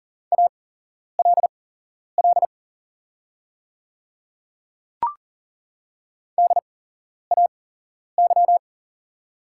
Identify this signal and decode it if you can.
Morse code, heard as a keyed tone, about 39 wpm, spaced as for 11 wpm, 690 Hz. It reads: ALL DAY
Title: 8x Character Spacing - Top 100 Words - 40wpm